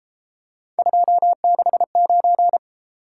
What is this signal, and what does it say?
Morse code, heard as a keyed tone, about 33 wpm, 700 Hz. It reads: 269